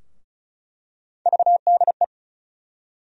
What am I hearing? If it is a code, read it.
Morse code, heard as a keyed tone, about 34 wpm, 700 Hz. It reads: VDE
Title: USA Call Sign Postfixes - 3 Letters 35wpm